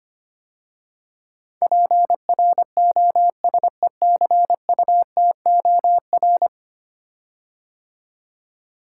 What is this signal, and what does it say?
Morse code, heard as a keyed tone, about 25 wpm, 690 Hz. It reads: PROSECUTOR
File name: 10 Letter Words - First 3 Letters Given - 25wpm